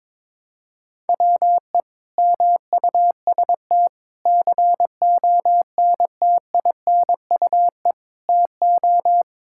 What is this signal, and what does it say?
Morse code, heard as a keyed tone, about 22 wpm, 700 Hz. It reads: WE MUST CONTINUE TO
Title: Sentences from Top 1000 Words - Review Part 6 - 22wpm